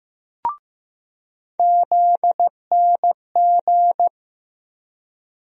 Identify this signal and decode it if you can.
Morse code, heard as a keyed tone, about 15 wpm, 700 Hz. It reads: ZNG
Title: USA Call Sign Postfixes - 3 Letters 15wpm